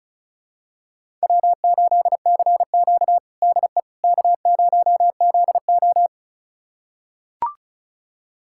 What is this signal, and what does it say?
Morse code, heard as a keyed tone, about 36 wpm, 700 Hz. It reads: W8CQ DE K0ZO
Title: USA Callsign DE Callsign - 4 Characters - 35wpm